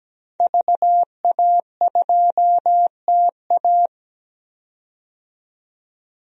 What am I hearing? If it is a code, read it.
Morse code, heard as a keyed tone, about 17 wpm, 700 Hz. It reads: VA2TA
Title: DXCC-100 Call Signs - 5 Characters - 17wpm